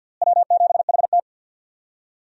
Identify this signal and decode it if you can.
Morse code, heard as a keyed tone, about 54 wpm, 700 Hz. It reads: W7HT
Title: USA Call Signs - 4 Characters - Repeated 3x - 50wpm